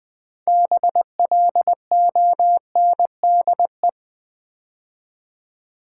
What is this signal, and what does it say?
Morse code, heard as a keyed tone, about 20 wpm, 690 Hz. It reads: BLONDE
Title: Build A Word - 3 Letters At A Time - 6-Letter Words - 20wpm